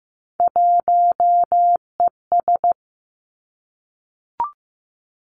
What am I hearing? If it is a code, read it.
Morse code, heard as a keyed tone, about 15 wpm, 700 Hz. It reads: 1ES
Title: Three Challenging Letter-Number Combinations 15wpm